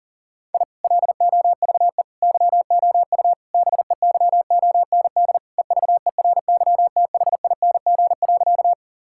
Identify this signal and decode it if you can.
Morse code, heard as a keyed tone, about 40 wpm, 700 Hz. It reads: I LOVE YOU BEYOND EVERYTHING.
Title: Sentences from Top 1000 Words - Review Part 6 - 40wpm